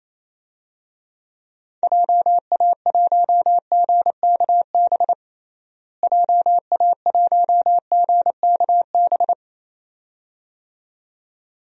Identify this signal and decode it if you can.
Morse code, heard as a keyed tone, about 27 wpm, 700 Hz. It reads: JA1GKB JA1GKB
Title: DXCC-100 Call Signs - 6 Characters - Repeated 2x - 28wpm